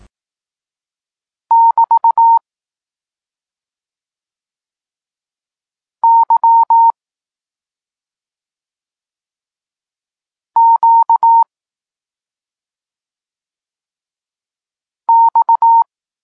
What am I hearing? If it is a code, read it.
Morse code, heard as a keyed tone, about 18 wpm, 920 Hz. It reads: <BT> Y Q X
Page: Tekst 32 – Doskonalenie odbioru poznanych liter